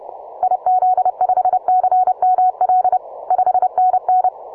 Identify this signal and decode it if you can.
Morse code, heard as a keyed tone, about 29 wpm, 740 Hz. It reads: IZ5CML 5NN